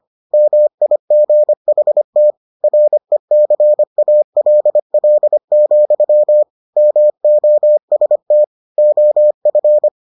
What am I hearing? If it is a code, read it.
Morse code, heard as a keyed tone, about 25 wpm, 600 Hz. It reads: MIGHT RECALL, MOST OF